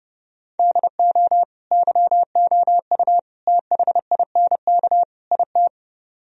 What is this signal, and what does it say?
Morse code, heard as a keyed tone, about 34 wpm, 700 Hz. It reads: DO YOU THINK IT